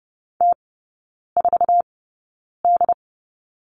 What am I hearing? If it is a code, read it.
Morse code, heard as a keyed tone, about 30 wpm, spaced as for 9 wpm, 690 Hz. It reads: T4D
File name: Lesson 17e - Letter u - Three Characters 30wpm